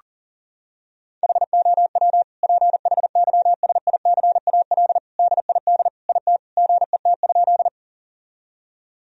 Morse code, heard as a keyed tone, about 37 wpm, 700 Hz. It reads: HOW PHYSICAL DID IT GET?
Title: Sentences from Top 1000 Words - Review Part 4 - 40wpm